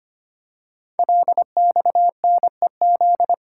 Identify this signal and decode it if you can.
Morse code, heard as a keyed tone, about 25 wpm, 700 Hz. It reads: LXNEZ
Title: Code Groups - 5 Letters - 25wpm